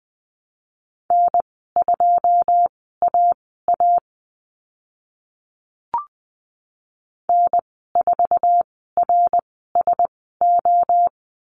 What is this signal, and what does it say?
Morse code, heard as a keyed tone, about 20 wpm, spaced as for 14 wpm, 700 Hz. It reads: N2AA N4RSO
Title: 082 Lesson 14g - Number 2 - Call Signs 20wpm